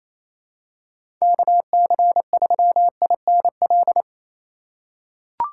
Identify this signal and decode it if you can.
Morse code, heard as a keyed tone, about 28 wpm, 700 Hz. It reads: KC3INL
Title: USA Call Signs - 6 Characters - 3x Character Spacing - 28wpm